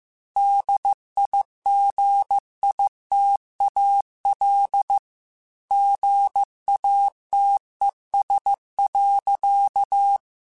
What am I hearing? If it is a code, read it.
Morse code, heard as a keyed tone, about 15 wpm, 800 Hz. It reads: DIGITAL GATES.
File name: Technical Subjects / Technical Subjects-mp3s / Gates Intro @13